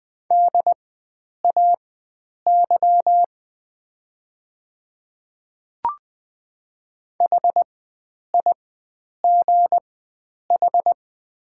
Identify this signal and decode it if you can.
Morse code, heard as a keyed tone, about 20 wpm, spaced as for 9 wpm, 690 Hz. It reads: DAY HIGH
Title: 4x Character Spacing - Top 300 Words - 20wpm